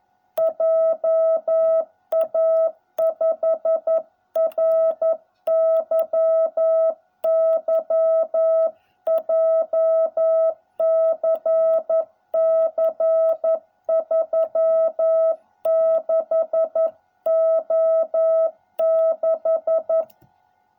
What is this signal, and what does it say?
Morse code, heard as a keyed tone, about 11 wpm, 660 Hz. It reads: JA5RYY JCC 36O6